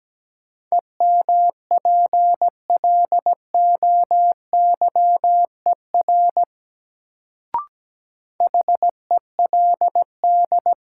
Morse code, heard as a keyed tone, about 17 wpm, 700 Hz. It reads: EMPLOYER HELD
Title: Sentences from Top 1600 Words - New Word Drill - 17wpm